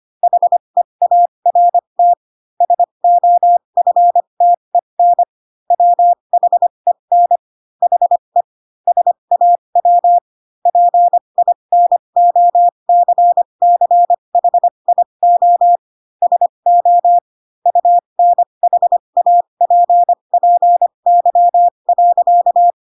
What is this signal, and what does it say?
Morse code, heard as a keyed tone, about 25 wpm, spaced as for 22 wpm, 700 Hz. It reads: HEART SOFTEN WHEN HE SAW PINOCCHIO SO UNHAPPY.